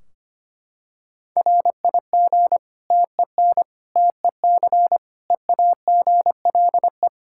Morse code, heard as a keyed tone, about 25 wpm, 700 Hz. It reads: RIG TEN TEC EAGLE
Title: Practice QSO - Exchange 2 - Info - 25wpm